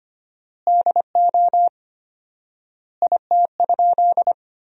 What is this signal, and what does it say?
Morse code, heard as a keyed tone, about 25 wpm, 700 Hz. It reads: DO IT?
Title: 4x Word-Spacing - Sentences from Top 100 Words - 25wpm